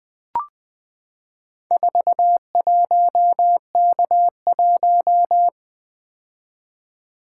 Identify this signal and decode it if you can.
Morse code, heard as a keyed tone, about 20 wpm, 700 Hz. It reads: 41K1
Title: Four Letter-Number Combinations - 20wpm